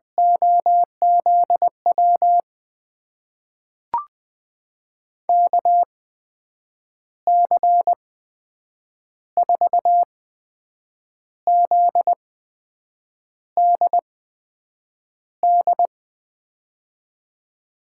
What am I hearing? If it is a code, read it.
Morse code, heard as a keyed tone, about 20 wpm, spaced as for 7 wpm, 700 Hz. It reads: OZW KC4ZDD
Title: USA Call Signs - 6 Characters - 8x Character Spacing - 20wpm